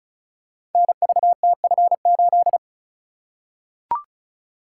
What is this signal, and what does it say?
Morse code, heard as a keyed tone, about 36 wpm, 700 Hz. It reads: NVTF8
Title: Code Groups - 5 Letter:Numbers - 35wpm